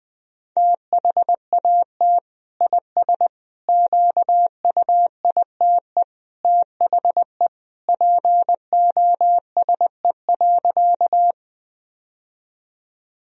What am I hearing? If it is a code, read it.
Morse code, heard as a keyed tone, about 20 wpm, 700 Hz. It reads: THAT IS QUITE THE POSE.
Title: Sentences from Top 1000 Words - Review Part 2 - 20wpm